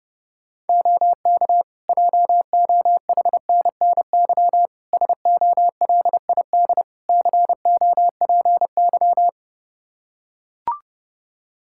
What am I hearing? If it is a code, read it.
Morse code, heard as a keyed tone, about 30 wpm, 700 Hz. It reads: OK JOHNNY SOLID COPY
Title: Practice QSO - Exchange 3 - More Info - 30wpm